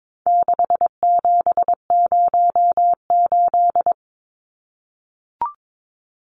Morse code, heard as a keyed tone, about 22 wpm, 700 Hz. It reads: 6708
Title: Numbers - 4-Digit 22wpm